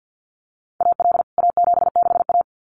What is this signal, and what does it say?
Morse code, heard as a keyed tone, about 49 wpm, 700 Hz. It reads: AL A76A
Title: CWT Exchange 50wpm